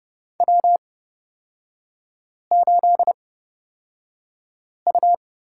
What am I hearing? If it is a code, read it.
Morse code, heard as a keyed tone, about 29 wpm, spaced as for 5 wpm, 690 Hz. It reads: W8U
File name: course / 30:10wpm / 236 Lesson 38e - Letter q - Three Characters 30wpm